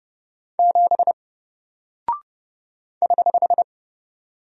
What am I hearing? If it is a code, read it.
Morse code, heard as a keyed tone, about 30 wpm, 700 Hz. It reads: 7 <HH>
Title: Single Character (Letter:Number:Punctuation:Prosign) Alt 30wpm